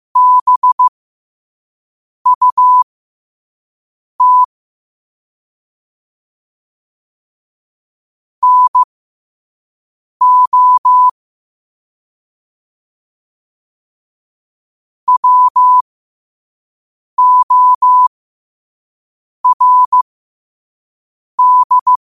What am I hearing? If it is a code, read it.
Morse code, heard as a keyed tone, about 15 wpm, spaced as for 5 wpm, 1000 Hz. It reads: BUT NO WORD
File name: qotdfiles / 5wpm